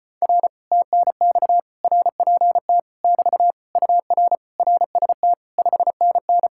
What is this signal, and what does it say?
Morse code, heard as a keyed tone, about 33 wpm, 700 Hz. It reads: R TNX RPT <BT> UR RST 5NN